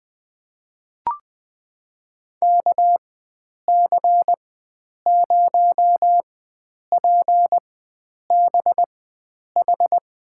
Morse code, heard as a keyed tone, about 20 wpm, spaced as for 9 wpm, 700 Hz. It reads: KC0PBH
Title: USA Call Signs - 6 Characters - 4x Character Spacing - 20wpm